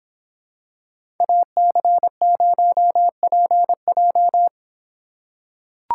Morse code, heard as a keyed tone, about 25 wpm, 700 Hz. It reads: AC0PJ